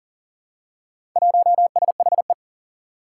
Morse code, heard as a keyed tone, about 39 wpm, 700 Hz. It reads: 1SHE
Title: Four Challenging Letter-Number Combinations 40wpm